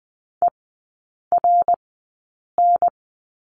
Morse code, heard as a keyed tone, about 20 wpm, spaced as for 8 wpm, 700 Hz. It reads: ERN
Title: 093 Lesson 16e - Letter c - Three Characters 20wpm